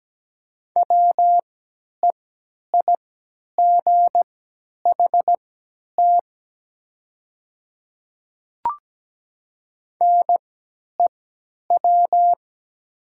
Morse code, heard as a keyed tone, about 17 wpm, spaced as for 10 wpm, 700 Hz. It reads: WEIGHT NEW
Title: 3x Character Spacing - Top 500 Words - 17wpm